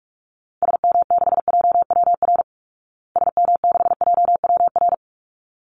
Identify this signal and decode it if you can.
Morse code, heard as a keyed tone, about 44 wpm, 700 Hz. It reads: SM6JWR SM6JWR
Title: Call Signs Repeated 3x - 45wpm